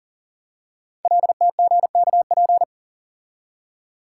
Morse code, heard as a keyed tone, about 39 wpm, 700 Hz. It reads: LTGKP
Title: Code Groups - 5 Letter:Numbers - 40wpm